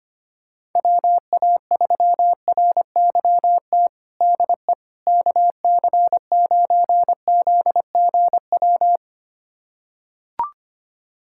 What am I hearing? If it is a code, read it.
Morse code, heard as a keyed tone, about 25 wpm, 700 Hz. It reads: WA3RYT DE KC9ZGW
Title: USA Callsign DE Callsign 25wpm